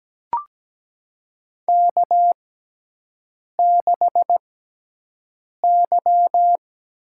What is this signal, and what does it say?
Morse code, heard as a keyed tone, about 17 wpm, spaced as for 6 wpm, 700 Hz. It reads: K6Y
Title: USA Call Signs - 4 Characters - 6x Character Spacing - 17wpm